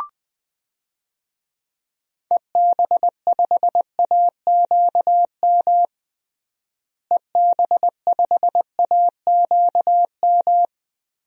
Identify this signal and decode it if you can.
Morse code, heard as a keyed tone, about 20 wpm, 700 Hz. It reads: EB5AQM EB5AQM
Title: DXCC-100 Call Signs - 6 Characters - Repeated 3x - 20wpm